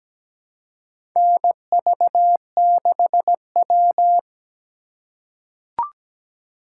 Morse code, heard as a keyed tone, about 17 wpm, 700 Hz. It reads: NV6W
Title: USA Call Signs - 4 Characters - 2x Character Spacing - 17wpm